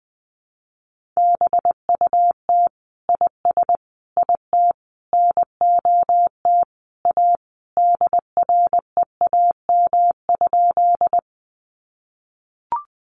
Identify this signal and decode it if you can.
Morse code, heard as a keyed tone, about 20 wpm, 700 Hz. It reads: BUT IS IT NOT A DREAM?
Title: Sentences from Top 1500 Words 20wpm